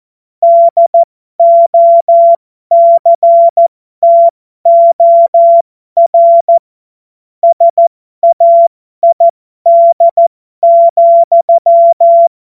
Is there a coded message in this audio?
Yes: Morse code from 0.4 to 12.3 s, heard as a keyed tone, about 14 wpm, spaced as for 12 wpm, 680 Hz. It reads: DOCTOR SAID,